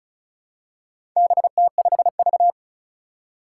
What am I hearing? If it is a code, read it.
Morse code, heard as a keyed tone, about 36 wpm, 700 Hz. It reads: BT5V
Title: Four Challenging Letter-Number Combinations 35wpm